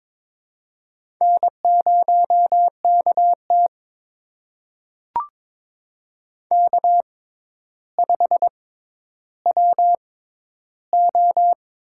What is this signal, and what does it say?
Morse code, heard as a keyed tone, about 22 wpm, spaced as for 11 wpm, 700 Hz. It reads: N0KT K5WO
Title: USA Call Signs - 4 Characters - 6x Character Spacing - 22wpm